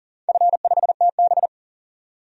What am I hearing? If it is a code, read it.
Morse code, heard as a keyed tone, about 42 wpm, 700 Hz. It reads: F5TB
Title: DXCC-100 Call Signs - 4 Characters - 40wpm